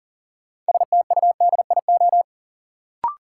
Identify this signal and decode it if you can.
Morse code, heard as a keyed tone, about 40 wpm, 700 Hz. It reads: STUDIO
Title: Sentences from Top 1200 Words - New Word Drill - 40wpm